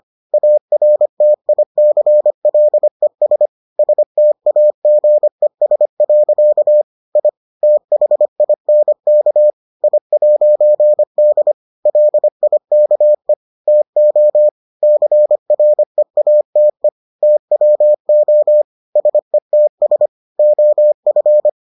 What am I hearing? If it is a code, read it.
Morse code, heard as a keyed tone, about 25 wpm, 600 Hz. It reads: ARTICLES STAGES. I THINK I'D LIKE TO CREATE TWO SETS OF